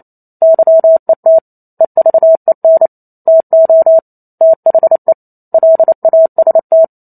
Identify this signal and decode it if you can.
Morse code, heard as a keyed tone, about 28 wpm, 650 Hz. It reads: YET EVEN TO THE LAST